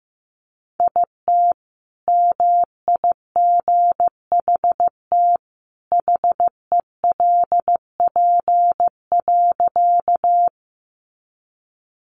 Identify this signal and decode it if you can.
Morse code, heard as a keyed tone, about 15 wpm, 700 Hz. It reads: IT MIGHT HELP.
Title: Sentences from Top 300 Words 15wpm